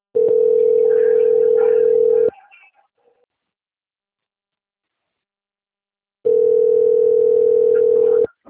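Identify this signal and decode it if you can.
Ringback tone (North America), 440 + 480 Hz.
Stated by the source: Outbound call recorded on a phone system labelled Cosmopolis - SP.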